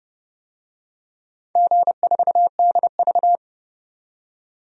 Morse code, heard as a keyed tone, about 30 wpm, 700 Hz. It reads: G4DV
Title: DXCC-100 Call Signs - 4 Characters - 30wpm